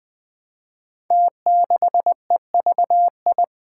Morse code, heard as a keyed tone, about 20 wpm, 700 Hz. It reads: T6EVI
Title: Five Challenging Letter-Number Combinations 20wpm